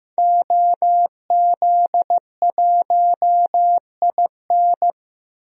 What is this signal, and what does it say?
Morse code, heard as a keyed tone, about 15 wpm, 700 Hz. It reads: OZ1IN
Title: DXCC-100 Call Signs - 5 Characters - 15wpm